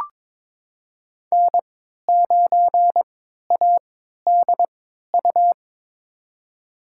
Morse code, heard as a keyed tone, about 21 wpm, spaced as for 12 wpm, 700 Hz. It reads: N9ADU
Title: USA Call Signs - 5 Characters - 3x Character Spacing - 22wpm